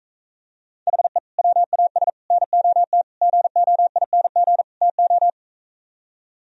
Morse code, heard as a keyed tone, about 45 wpm, 700 Hz. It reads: HE WAS NOT GOING TO